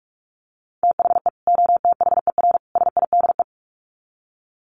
Morse code, heard as a keyed tone, about 44 wpm, 700 Hz. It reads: THE OTHER SIDE
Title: Sets of 3 Words 45wpm